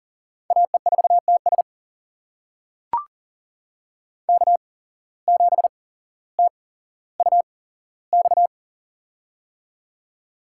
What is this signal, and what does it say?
Morse code, heard as a keyed tone, about 40 wpm, spaced as for 17 wpm, 700 Hz. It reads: AE4TS K7TUX
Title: USA Call Signs - 5 Characters - 8x Character Spacing - 40wpm